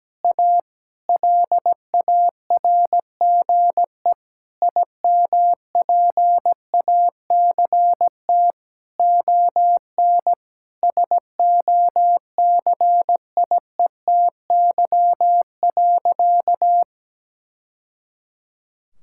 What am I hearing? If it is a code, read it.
Morse code, heard as a keyed tone, about 17 wpm, 700 Hz. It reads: A LARGE IMPACT ON SOCIETY.
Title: Sentences from Top 1000 Words - Review Part 6 - 17wpm